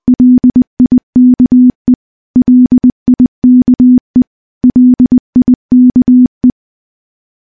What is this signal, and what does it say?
Morse code, heard as a keyed tone, about 20 wpm, 250 Hz. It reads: LIKE LIKE LIKE
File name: morse-training-20WPM